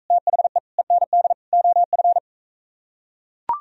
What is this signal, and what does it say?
Morse code, heard as a keyed tone, about 43 wpm, 700 Hz. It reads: THE END OF